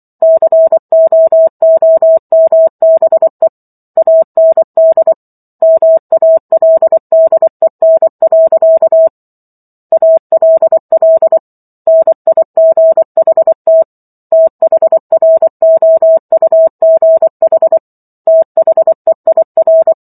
Morse code, heard as a keyed tone, about 24 wpm, 650 Hz. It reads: COOMBE AND MALDEN. ALL NIGHT THROUGH THEIR